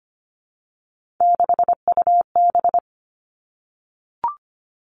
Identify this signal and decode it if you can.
Morse code, heard as a keyed tone, about 25 wpm, 700 Hz. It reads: 6UB
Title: Three Challenging Letter-Number Combinations 25wpm